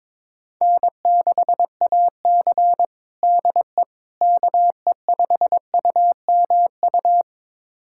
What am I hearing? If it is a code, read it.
Morse code, heard as a keyed tone, about 22 wpm, 700 Hz. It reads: N6AC DE KE5UMU
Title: USA Callsign DE Callsign 22wpm